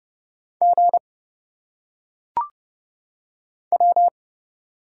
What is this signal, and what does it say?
Morse code, heard as a keyed tone, about 30 wpm, 700 Hz. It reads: G W
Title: Single Letters - Phonetic Pronunciation 30wpm